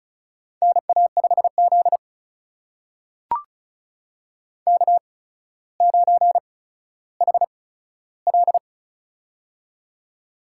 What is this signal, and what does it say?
Morse code, heard as a keyed tone, about 36 wpm, spaced as for 15 wpm, 700 Hz. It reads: NA5Z K9HL
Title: USA Call Signs - 4 Characters - 8x Character Spacing - 35wpm